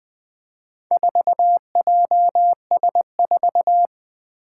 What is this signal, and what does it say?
Morse code, heard as a keyed tone, about 20 wpm, 700 Hz. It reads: 4JS4
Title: Four Challenging Letter-Number Combinations 20wpm